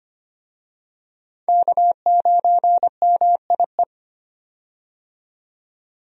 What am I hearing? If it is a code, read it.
Morse code, heard as a keyed tone, about 25 wpm, 700 Hz. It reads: K9MIE
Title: USA Call Signs - 5 Characters - 25wpm